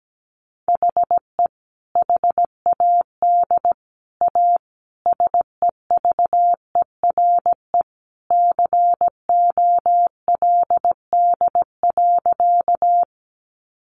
Morse code, heard as a keyed tone, about 17 wpm, 700 Hz. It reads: HE HAD A SEVERE COLD.
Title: Sentences from Top 1400 Words 17wpm